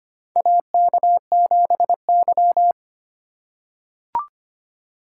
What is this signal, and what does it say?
Morse code, heard as a keyed tone, about 25 wpm, 700 Hz. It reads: AK7Y
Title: USA Call Signs - 3x Character Spacing - 25wpm